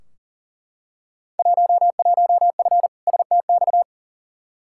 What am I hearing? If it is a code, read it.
Morse code, heard as a keyed tone, about 40 wpm, 700 Hz. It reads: 11F STX
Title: Field Day Exchanges Encore - 40wpm